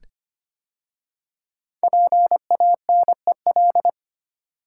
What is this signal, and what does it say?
Morse code, heard as a keyed tone, about 25 wpm, 700 Hz. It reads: PANEL
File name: 5 Letter Words - First 3 Letters Given - 25wpm